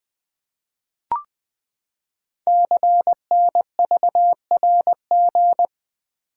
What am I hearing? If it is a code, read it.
Morse code, heard as a keyed tone, about 20 wpm, 700 Hz. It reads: CNVRG
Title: Code Groups - 5 Letters - 20wpm